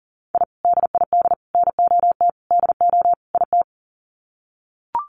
Morse code, heard as a keyed tone, about 40 wpm, 700 Hz. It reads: I DID NOT DO IT
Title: Sets of 5 Words Encore2 40wpm